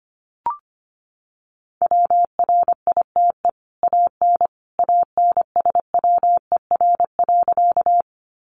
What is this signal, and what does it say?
Morse code, heard as a keyed tone, about 25 wpm, 700 Hz. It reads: WRITE AN ANSWER.
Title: Sentences from Top 300 Words 25wpm